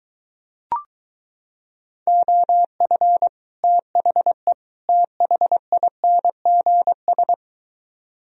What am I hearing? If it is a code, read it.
Morse code, heard as a keyed tone, about 23 wpm, 700 Hz. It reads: OF THE THINGS